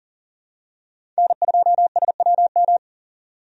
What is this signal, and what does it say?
Morse code, heard as a keyed tone, about 40 wpm, 700 Hz. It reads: N2SWM
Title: USA Call Signs - 5 Characters - 40wpm